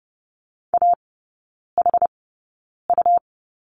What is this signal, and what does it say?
Morse code, heard as a keyed tone, about 30 wpm, spaced as for 9 wpm, 700 Hz. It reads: AHU
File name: Lesson 17e - Letter u - Three Characters 30wpm